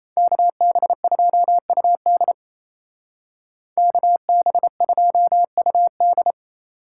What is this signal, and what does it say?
Morse code, heard as a keyed tone, about 36 wpm, 700 Hz. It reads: KB2UD KB2UD